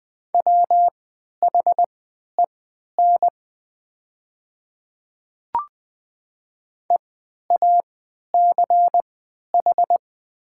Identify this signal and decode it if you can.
Morse code, heard as a keyed tone, about 20 wpm, spaced as for 11 wpm, 700 Hz. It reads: WHEN EACH
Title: 3x Character Spacing - Top 200 Words - 20wpm